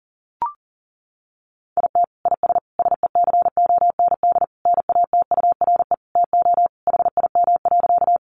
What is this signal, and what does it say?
Morse code, heard as a keyed tone, about 40 wpm, 700 Hz. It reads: IT IS SECOND NATURE TO HIM.
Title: Sentences from Top 600 Words 40wpm